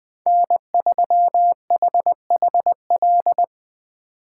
Morse code, heard as a keyed tone, about 23 wpm, 700 Hz. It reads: N3HHL
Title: Speed Racing - USA 1x3 Call Signs 20wpm